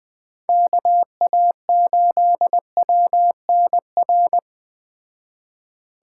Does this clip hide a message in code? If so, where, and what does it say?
Yes: Morse code from 0.5 to 4.4 s, heard as a keyed tone, about 21 wpm, 700 Hz. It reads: KA8WNR